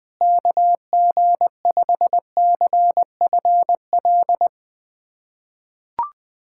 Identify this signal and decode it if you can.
Morse code, heard as a keyed tone, about 20 wpm, 700 Hz. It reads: KG5CFL
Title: USA Callsign DE Callsign - 6 Characters - 20wpm